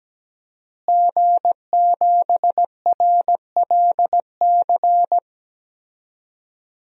Morse code, heard as a keyed tone, about 17 wpm, 700 Hz. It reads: G7RLC
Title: DXCC-100 Call Signs - 5 Characters - Repeated 2x - 17wpm